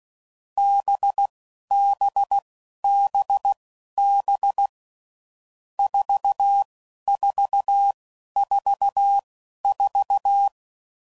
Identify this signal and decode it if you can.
Morse code, heard as a keyed tone, about 16 wpm, spaced as for 11 wpm, 800 Hz. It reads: BBBB 4444